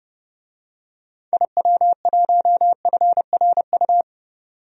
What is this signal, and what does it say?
Morse code, heard as a keyed tone, about 32 wpm, 700 Hz. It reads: IW1FRU
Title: DXCC-100 Call Signs - 6 Characters - 30wpm